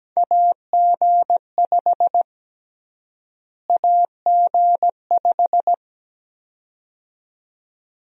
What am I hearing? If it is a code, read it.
Morse code, heard as a keyed tone, about 17 wpm, 700 Hz. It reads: AG5 AG5
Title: USA Call Sign Prefixes - Repeated 2x - 17wpm